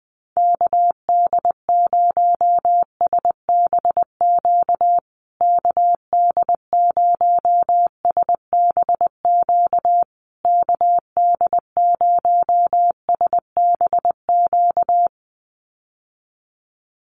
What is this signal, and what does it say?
Morse code, heard as a keyed tone, about 20 wpm, 700 Hz. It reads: KD0SBQ KD0SBQ KD0SBQ
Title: USA Call Signs Repeated 3x 20wpm